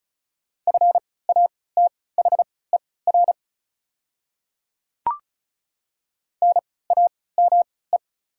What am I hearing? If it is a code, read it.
Morse code, heard as a keyed tone, about 36 wpm, spaced as for 20 wpm, 690 Hz. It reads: FATHER NAME
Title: 3x Character Spacing - Top 200 Words - 35wpm